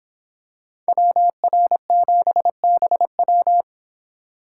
Morse code, heard as a keyed tone, about 26 wpm, 700 Hz. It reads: WR7BW